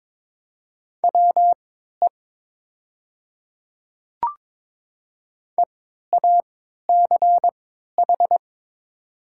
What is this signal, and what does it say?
Morse code, heard as a keyed tone, about 22 wpm, spaced as for 13 wpm, 690 Hz. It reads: WE EACH
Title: 3x Character Spacing - Top 200 Words - 22wpm